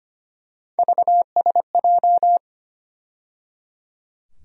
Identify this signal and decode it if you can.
Morse code, heard as a keyed tone, about 25 wpm, 700 Hz. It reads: VSJ
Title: USA Call Sign Postfixes - 3 Letters 25wpm